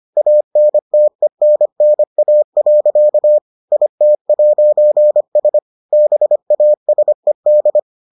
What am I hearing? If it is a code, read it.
Morse code, heard as a keyed tone, about 25 wpm, 600 Hz. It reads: ANTENNA. IT'S BASED